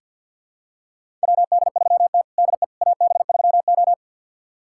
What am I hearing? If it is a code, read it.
Morse code, heard as a keyed tone, about 47 wpm, 690 Hz. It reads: WD3T DE AB3O
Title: USA Callsign DE Callsign - 4 Characters - 50wpm